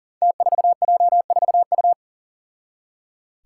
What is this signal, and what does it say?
Morse code, heard as a keyed tone, about 37 wpm, 700 Hz. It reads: T4J4U
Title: Five Challenging Letter-Number Combinations 40wpm